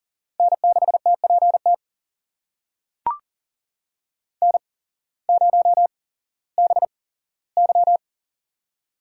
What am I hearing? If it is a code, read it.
Morse code, heard as a keyed tone, about 40 wpm, spaced as for 19 wpm, 690 Hz. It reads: N6TPT N0BY
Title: USA Call Signs - 8x Character Spacing - 40wpm